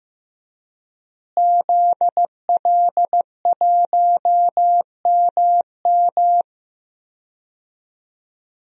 Morse code, heard as a keyed tone, about 16 wpm, 700 Hz. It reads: ZL1MM
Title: DXCC-100 Call Signs - 5 Characters - Repeated 3x - 15wpm